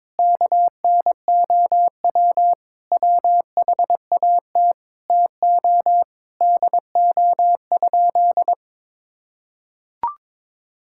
Morse code, heard as a keyed tone, about 22 wpm, 700 Hz. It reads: KNOW WHAT TO DO?
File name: practice / 22wpm / Sentences with 2-Word Phrases - 22wpm